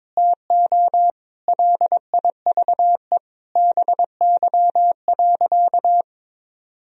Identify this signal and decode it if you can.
Morse code, heard as a keyed tone, about 22 wpm, 700 Hz. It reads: TO LIVE BY.
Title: Sentences from Top 1000 Words - Review Part 3 - 22wpm